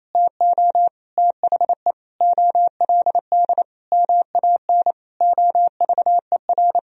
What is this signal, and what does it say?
Morse code, heard as a keyed tone, about 27 wpm, 700 Hz. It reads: TO THE OLD MAN OVER
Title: Sentences from Top 200 Words and Abbreviations 28wpm